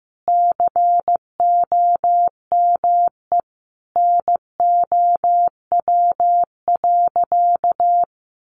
Morse code, heard as a keyed tone, about 15 wpm, 700 Hz. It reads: COME NOW.
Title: Sentences from Top 200 Words 15wpm